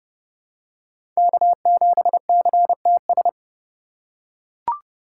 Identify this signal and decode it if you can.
Morse code, heard as a keyed tone, about 30 wpm, 700 Hz. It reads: K7CTS
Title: USA Call Signs - 5 Characters - 2x Character Spacing - 30wpm